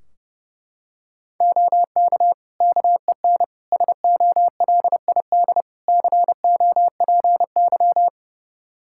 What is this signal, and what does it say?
Morse code, heard as a keyed tone, about 32 wpm, 700 Hz. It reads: OK KEN SOLID COPY
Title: Practice QSO - Exchange 3 - More Info - 30wpm